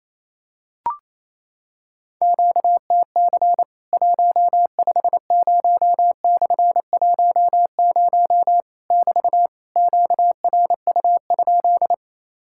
Morse code, heard as a keyed tone, about 28 wpm, 700 Hz. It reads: QTC 150/10 <BT> QRU?
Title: Worked All Europe DX Contest - QTC Exchange - 28wpm